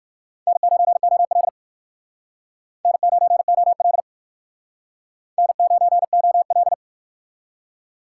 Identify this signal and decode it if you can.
Morse code, heard as a keyed tone, about 59 wpm, 700 Hz. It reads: N9OL N9OL N9OL